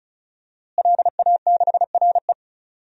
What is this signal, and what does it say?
Morse code, heard as a keyed tone, about 34 wpm, 700 Hz. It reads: LA6RE
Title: DXCC-100 Call Signs - 5 Characters - Repeated 3x - 35wpm